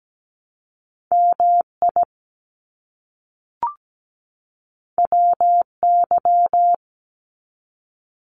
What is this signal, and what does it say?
Morse code, heard as a keyed tone, about 17 wpm, 700 Hz. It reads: MI WY
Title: US States 17wpm